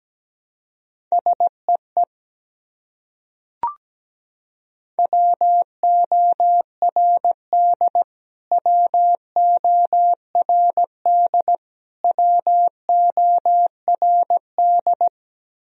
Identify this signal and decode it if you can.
Morse code, heard as a keyed tone, about 17 wpm, 700 Hz. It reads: SEE WORD WORD WORD
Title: Top 100 Words Repeated 17wpm